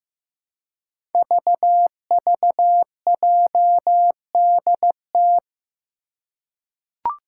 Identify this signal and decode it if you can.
Morse code, heard as a keyed tone, about 15 wpm, 700 Hz. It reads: VVJDT
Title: Five Challenging Letter-Number Combinations 15wpm